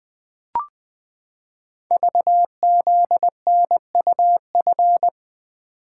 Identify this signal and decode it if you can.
Morse code, heard as a keyed tone, about 20 wpm, 700 Hz. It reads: VZNUF
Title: Code Groups - 5 Letter:Numbers - 20wpm